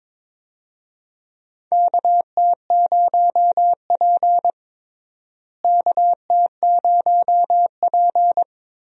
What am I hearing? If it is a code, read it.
Morse code, heard as a keyed tone, about 22 wpm, 700 Hz. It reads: KT0P KT0P
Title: USA Call Signs - 4 Characters - Repeated 3x - 22wpm